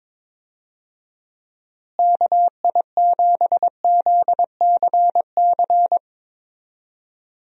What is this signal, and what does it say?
Morse code, heard as a keyed tone, about 22 wpm, 700 Hz. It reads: KI7ZCC
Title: USA Call Signs - 6 Characters - Repeated 2x - 22wpm